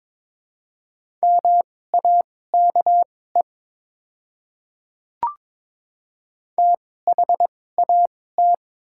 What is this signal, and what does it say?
Morse code, heard as a keyed tone, about 22 wpm, spaced as for 16 wpm, 700 Hz. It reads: MAKE THAT
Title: 2x Character Spacing - Top 100 Words - 22wpm